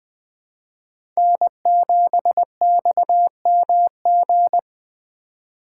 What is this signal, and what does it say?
Morse code, heard as a keyed tone, about 19 wpm, 700 Hz. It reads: N7XMG